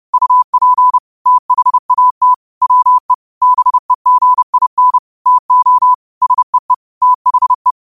Morse code, heard as a keyed tone, about 30 wpm, 1000 Hz. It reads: UP THAT WE BEGIN TO SEE THE